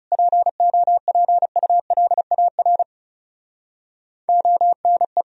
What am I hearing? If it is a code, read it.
Morse code, heard as a keyed tone, about 37 wpm, 700 Hz. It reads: POPULAR ONE
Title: Speed-Racing - Sets of 5 Words 30wpm